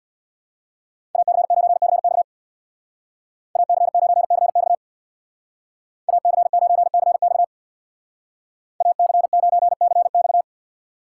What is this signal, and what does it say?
Morse code, heard as a keyed tone, about 62 wpm, 700 Hz. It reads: AC9CX AC9CX AC9CX AC9CX
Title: Speed Racing - USA 2x2 Call Signs 50wpm